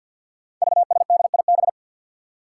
Morse code, heard as a keyed tone, about 51 wpm, 700 Hz. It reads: VSDIB
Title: Five Challenging Letter-Number Combinations 50wpm